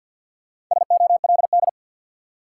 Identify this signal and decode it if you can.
Morse code, heard as a keyed tone, about 47 wpm, 700 Hz. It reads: SOLD
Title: Sentences from Top 1400 Words - New Word Drill - 50wpm